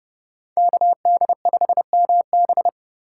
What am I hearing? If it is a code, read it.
Morse code, heard as a keyed tone, about 30 wpm, 700 Hz. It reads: KD5MB
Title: USA Call Signs - 5 Characters - 4x Character Spacing - 30wpm